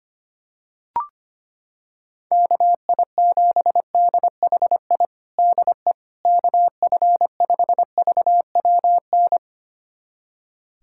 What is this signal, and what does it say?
Morse code, heard as a keyed tone, about 25 wpm, 700 Hz. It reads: KI7DHI DE KF5VWN
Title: USA Callsign DE Callsign - 6 Characters - 25wpm